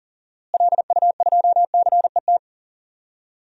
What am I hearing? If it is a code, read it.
Morse code, heard as a keyed tone, about 40 wpm, 700 Hz. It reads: LU2CET
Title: DXCC-100 Call Signs - 6 Characters - Repeated 2x - 40wpm